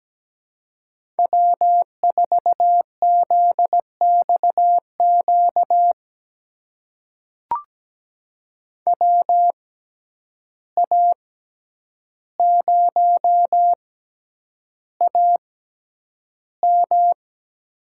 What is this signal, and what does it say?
Morse code, heard as a keyed tone, about 17 wpm, spaced as for 9 wpm, 700 Hz. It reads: W4ZXQ WA0AM
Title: USA Call Signs - 5 Characters - 6x Character Spacing - 17wpm